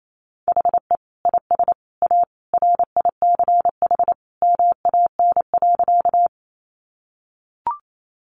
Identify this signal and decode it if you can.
Morse code, heard as a keyed tone, about 28 wpm, 700 Hz. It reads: HE IS A RICH MAN.
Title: Sentences from Top 1100 Words 28wpm